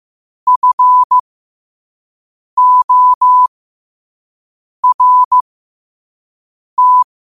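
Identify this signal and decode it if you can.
Morse code, heard as a keyed tone, about 15 wpm, spaced as for 5 wpm, 1000 Hz. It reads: FORT
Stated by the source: Morse at 5 wpm.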